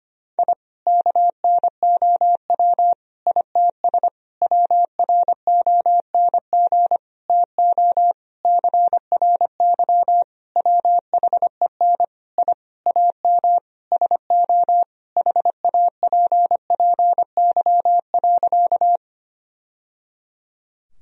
Morse code, heard as a keyed tone, about 25 wpm, 700 Hz. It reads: I KNOW ITS WRONG TO CRY WHEN I AM SO HAPPY.
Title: Sentences from Top 1000 Words - Review Part 2 - 25wpm